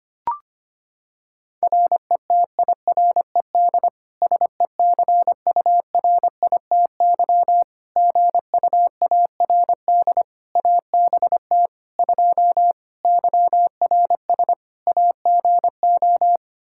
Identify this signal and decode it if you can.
Morse code, heard as a keyed tone, about 25 wpm, 700 Hz. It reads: RETIRED SECURITY GUARD ABT 2 YRS AGO
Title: Practice QSO - Exchange 3 - More Info - 25wpm